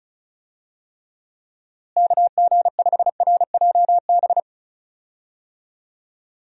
Morse code, heard as a keyed tone, about 36 wpm, 700 Hz. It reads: KG5RJB
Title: USA Call Signs - 6 Characters - Repeated 3x - 35wpm